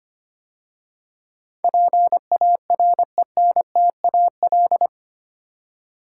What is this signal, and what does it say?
Morse code, heard as a keyed tone, about 25 wpm, 700 Hz. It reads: PARENTAL
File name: 8 Letter Words - First 3 Letters Given - 25wpm